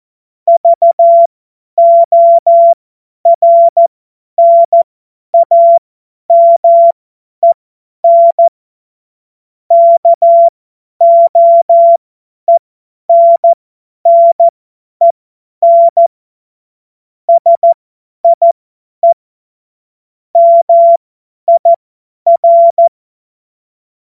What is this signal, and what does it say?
Morse code, heard as a keyed tone, about 14 wpm, spaced as for 10 wpm, 680 Hz. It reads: VORNAMEN KOENNEN SIE MIR
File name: Witz-28-Oktober-2025-50-F70